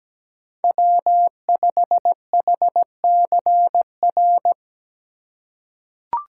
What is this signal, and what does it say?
Morse code, heard as a keyed tone, about 17 wpm, 700 Hz. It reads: W5HCR
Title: USA Call Signs - 5 Characters - 2x Character Spacing - 17wpm